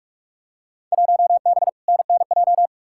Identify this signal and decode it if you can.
Morse code, heard as a keyed tone, about 46 wpm, 700 Hz. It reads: 1B NNJ
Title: Field Day Exchanges Encore - 45wpm